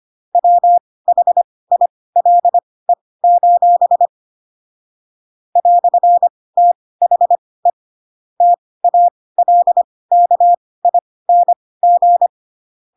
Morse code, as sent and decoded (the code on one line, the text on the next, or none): .-- .... .. .-.. ---... / .-..-. - .... / - .- .-.. -.- .. -. --.
WHILE: "THE TALKING